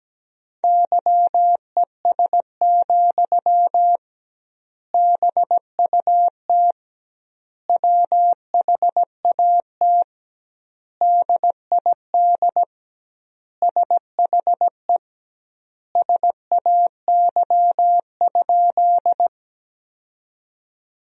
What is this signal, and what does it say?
Morse code, heard as a keyed tone, about 17 wpm, 700 Hz. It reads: YES, BUT WHAT DID SHE SAY?
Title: 2x Word-Spacing - Sentences from Top 500 Words - 17wpm